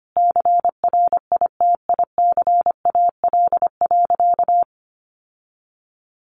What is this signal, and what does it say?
Morse code, heard as a keyed tone, about 25 wpm, 700 Hz. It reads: CRITICAL.
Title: Sentences from Top 1100 Words 25wpm